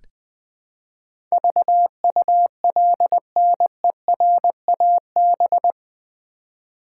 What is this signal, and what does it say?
Morse code, heard as a keyed tone, about 20 wpm, 700 Hz. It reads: VULNERAB
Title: Build A Word - 4 Letters At A Time - 10-Letter Words - 20wpm